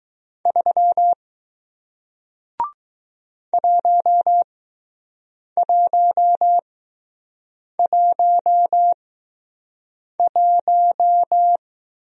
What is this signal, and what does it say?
Morse code, heard as a keyed tone, about 20 wpm, spaced as for 6 wpm, 700 Hz. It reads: 3 1111